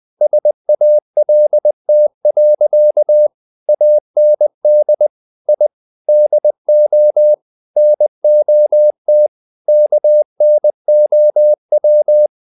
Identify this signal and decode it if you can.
Morse code, heard as a keyed tone, about 20 wpm, 600 Hz. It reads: SALT. AND I DO NOT KNOW
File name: the old man and the sea / the old man and the sea part 1 20wpm